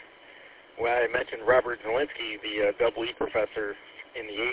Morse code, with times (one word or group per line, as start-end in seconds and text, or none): none